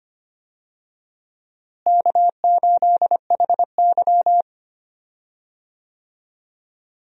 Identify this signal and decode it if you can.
Morse code, heard as a keyed tone, about 25 wpm, 700 Hz. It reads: K8HY
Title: USA Call Signs - 4 Characters - Repeated 2x - 25wpm